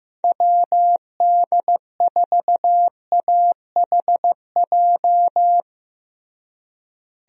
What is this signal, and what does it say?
Morse code, heard as a keyed tone, about 15 wpm, 700 Hz. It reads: WD4AHJ
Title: USA Call Signs - 6 Characters - Repeated 2x - 15wpm